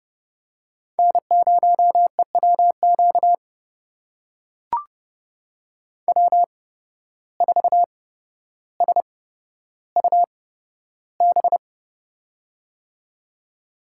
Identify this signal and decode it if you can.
Morse code, heard as a keyed tone, about 30 wpm, spaced as for 13 wpm, 700 Hz. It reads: N0EWQ W4SUB
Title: USA Call Signs - 5 Characters - 8x Character Spacing - 30wpm